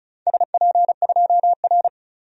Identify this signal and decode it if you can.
Morse code, heard as a keyed tone, about 34 wpm, 700 Hz. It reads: SP2R
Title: DXCC-100 Call Signs - 4 Characters - 35wpm